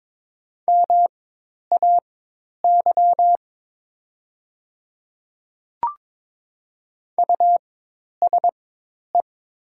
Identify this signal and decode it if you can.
Morse code, heard as a keyed tone, about 22 wpm, spaced as for 10 wpm, 700 Hz. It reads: MAY USE
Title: 4x Character Spacing - Top 300 Words - 22wpm